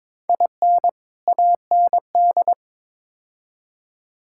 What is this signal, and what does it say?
Morse code, heard as a keyed tone, about 22 wpm, 700 Hz. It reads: IN AND
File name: practice / 22wpm / Build A Phrase - 5-Word Phrases - 22wpm